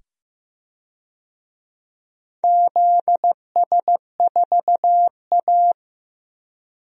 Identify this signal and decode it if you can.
Morse code, heard as a keyed tone, about 15 wpm, 700 Hz. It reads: ZS4A
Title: DXCC-100 Call Signs - 4 Characters - 15wpm